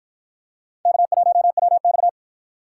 Morse code, heard as a keyed tone, about 47 wpm, 700 Hz. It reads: K1WX